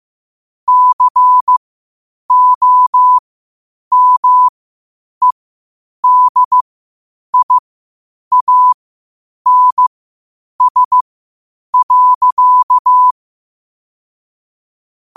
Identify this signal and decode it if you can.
Morse code, heard as a keyed tone, about 15 wpm, spaced as for 8 wpm, 1000 Hz. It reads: COMEDIANS.